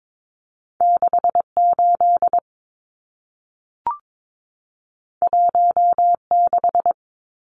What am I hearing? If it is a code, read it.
Morse code, heard as a keyed tone, about 22 wpm, 700 Hz. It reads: 68 16
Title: Numbers - 2-Digit 22wpm